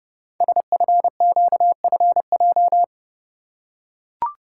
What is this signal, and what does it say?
Morse code, heard as a keyed tone, about 30 wpm, 700 Hz. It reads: SFQFJ
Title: Code Groups - 5 Letters - 30wpm